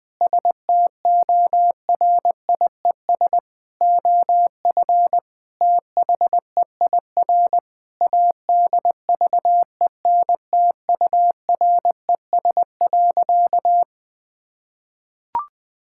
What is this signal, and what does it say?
Morse code, heard as a keyed tone, about 20 wpm, 700 Hz. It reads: STORIES OF THEIR ADVENTURES.
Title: Sentences with 2-Word Phrases - 20wpm